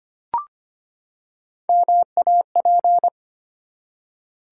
Morse code, heard as a keyed tone, about 25 wpm, 700 Hz. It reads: MAP
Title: Three Letter Words - 25wpm